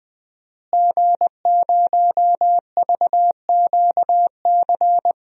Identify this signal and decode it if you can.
Morse code, heard as a keyed tone, about 21 wpm, 700 Hz. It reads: G0VQC
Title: DXCC-100 Call Signs - 5 Characters - Repeated 2x - 20wpm